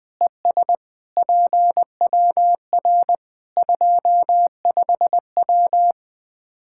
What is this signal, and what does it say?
Morse code, heard as a keyed tone, about 21 wpm, 700 Hz. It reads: ES PWR 25W
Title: Practice QSO - Exchange 2 - Info - 20wpm